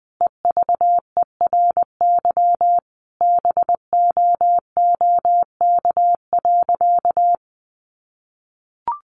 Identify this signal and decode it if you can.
Morse code, heard as a keyed tone, about 20 wpm, 700 Hz. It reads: EVERY BOOK.
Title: Sentences from Top 1500 Words 20wpm